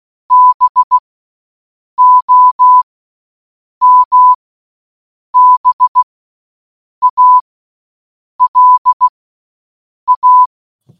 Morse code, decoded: BOMBALA